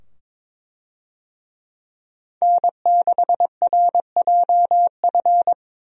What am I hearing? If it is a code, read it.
Morse code, heard as a keyed tone, about 22 wpm, 700 Hz. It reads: N6RJF
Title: USA Call Signs - 5 Characters - 22wpm